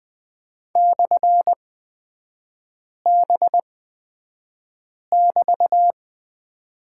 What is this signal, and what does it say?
Morse code, heard as a keyed tone, about 20 wpm, spaced as for 5 wpm, 700 Hz. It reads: /B<BT>
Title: 224 Lesson 36e - BT - Three Characters 20wpm